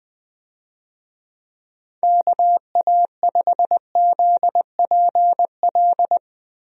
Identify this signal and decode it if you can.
Morse code, heard as a keyed tone, about 20 wpm, 700 Hz. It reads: KA5ZPL
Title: USA Call Signs - 6 Characters - Repeated 2x - 20wpm